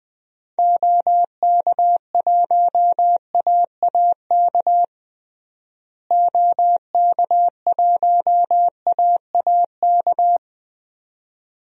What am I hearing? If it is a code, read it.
Morse code, heard as a keyed tone, about 20 wpm, 700 Hz. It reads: OK1AAK OK1AAK
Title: DXCC-100 Call Signs - 6 Characters - Repeated 2x - 20wpm